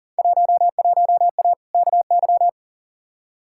Morse code, heard as a keyed tone, about 40 wpm, 700 Hz. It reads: 11A KY
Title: Field Day Exchanges Encore - 40wpm